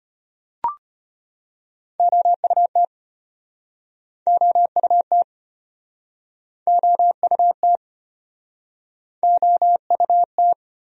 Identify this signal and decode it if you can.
Morse code, heard as a keyed tone, about 36 wpm, 700 Hz. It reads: OUT OUT OUT OUT